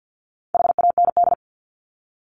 Morse code, heard as a keyed tone, about 50 wpm, 700 Hz. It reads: HAND
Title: ES:S Suffix - Top 100 Words with Base Words 50wpm